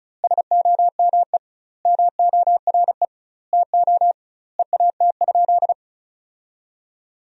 Morse code, heard as a keyed tone, about 34 wpm, 700 Hz. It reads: SOME MORE TO EAT?
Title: 2x Word-Spacing - Sentences from Top 300 Words - 35wpm